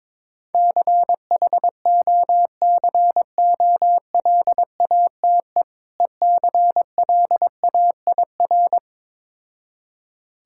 Morse code, heard as a keyed tone, about 22 wpm, 690 Hz. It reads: CHOCOLATE ECLAIR
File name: Difficult Words with Hints - 22wpm